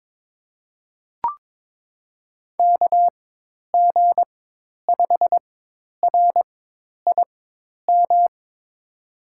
Morse code, heard as a keyed tone, about 22 wpm, spaced as for 10 wpm, 700 Hz. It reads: KG5RIM
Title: USA Call Signs - 6 Characters - 4x Character Spacing - 22wpm